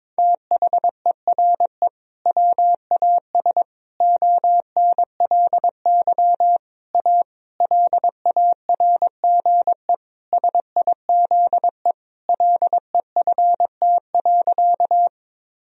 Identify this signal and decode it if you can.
Morse code, heard as a keyed tone, about 22 wpm, 700 Hz. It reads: THERE WAS ONLY A LARGE SIZE LEFT.
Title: Sentences from Top 1000 Words - Review Part 2 - 22wpm